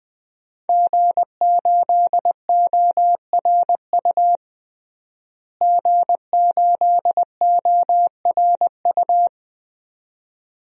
Morse code, heard as a keyed tone, about 21 wpm, 700 Hz. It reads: G8ORU G8ORU
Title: DXCC-100 Call Signs - 5 Characters - Repeated 2x - 20wpm